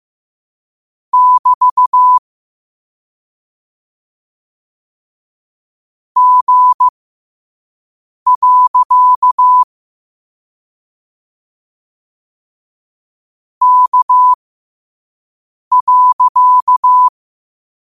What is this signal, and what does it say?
Morse code, heard as a keyed tone, about 15 wpm, spaced as for 5 wpm, 1000 Hz. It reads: <BT> G. K.
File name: qotdfiles / 5wpm